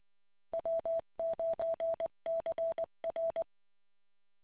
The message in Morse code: W9CR

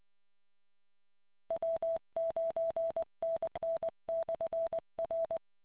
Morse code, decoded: W9NN/R

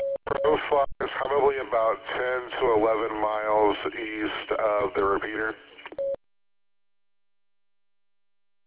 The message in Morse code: none

